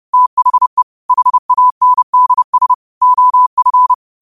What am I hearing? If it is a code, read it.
Morse code, heard as a keyed tone, about 30 wpm, 1000 Hz. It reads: THE HANDS OF